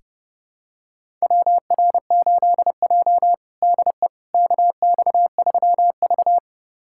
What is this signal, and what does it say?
Morse code, heard as a keyed tone, about 30 wpm, 700 Hz. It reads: WR8J DE KX3V
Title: USA Callsign DE Callsign - 4 Characters - 30wpm